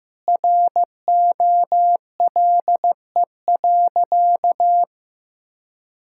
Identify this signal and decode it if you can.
Morse code, heard as a keyed tone, about 16 wpm, 700 Hz. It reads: ROLE.
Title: Sentences from Top 1000 Words - Review Part 6 - 15wpm